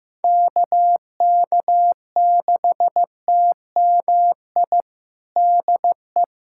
Morse code, heard as a keyed tone, about 15 wpm, 700 Hz. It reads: KK6TMI DE
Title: USA Callsign DE Callsign 15wpm